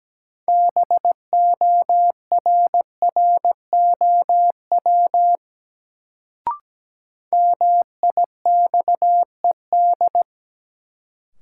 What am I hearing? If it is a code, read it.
Morse code, heard as a keyed tone, about 17 wpm, 700 Hz. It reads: BORROW MIXED
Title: Sentences from Top 1500 Words - New Word Drill - 17wpm